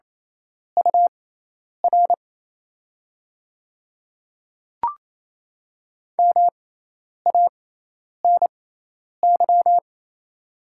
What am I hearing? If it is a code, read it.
Morse code, heard as a keyed tone, about 28 wpm, spaced as for 10 wpm, 700 Hz. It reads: UR MANY
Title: 6x Character Spacing - Top 200 Words - 28wpm